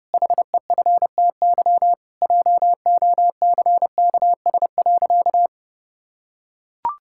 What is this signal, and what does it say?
Morse code, heard as a keyed tone, about 30 wpm, 700 Hz. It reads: HEFTY JOCKS.
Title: Pangram Challenge v2 - 30wpm